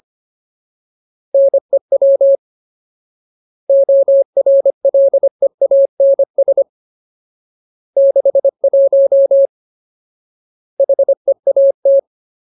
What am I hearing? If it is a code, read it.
Morse code, heard as a keyed tone, about 25 wpm, 550 Hz. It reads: NEW ORLEANS 61 HEAT
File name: Ultimate Word List / 25WPM Ultimate